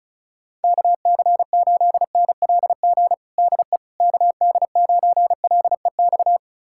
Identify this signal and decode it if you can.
Morse code, heard as a keyed tone, about 36 wpm, 700 Hz. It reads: KC8NLG DE KD9LEX
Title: USA Callsign DE Callsign - 6 Characters - 35wpm